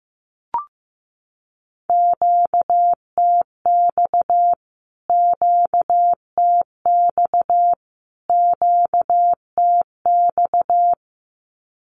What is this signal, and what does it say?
Morse code, heard as a keyed tone, about 15 wpm, 700 Hz. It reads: QTX QTX QTX
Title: Q-Codes Repeated 3x 15wpm